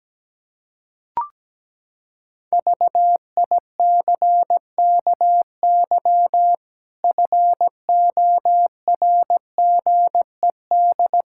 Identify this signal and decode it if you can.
Morse code, heard as a keyed tone, about 17 wpm, 700 Hz. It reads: VICKY FORGED
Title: Pangram Challenge v2 - 17wpm